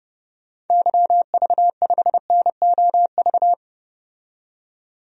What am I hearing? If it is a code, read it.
Morse code, heard as a keyed tone, about 32 wpm, 700 Hz. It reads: YV5NOV
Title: DXCC-100 Call Signs - 6 Characters - 30wpm